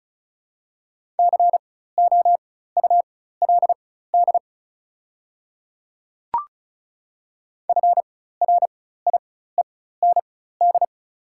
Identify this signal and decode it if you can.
Morse code, heard as a keyed tone, about 36 wpm, spaced as for 16 wpm, 700 Hz. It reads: COULD FRIEND
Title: 4x Character Spacing - Top 500 Words - 35wpm